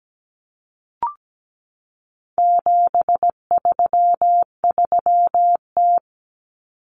Morse code, heard as a keyed tone, about 17 wpm, 700 Hz. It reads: 733T
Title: Common Cut Numbers - 4-Digit 17wpm